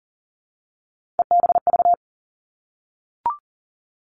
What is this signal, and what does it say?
Morse code, heard as a keyed tone, about 38 wpm, 700 Hz. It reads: EBV
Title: Three Challenging Letter-Number Combinations 40wpm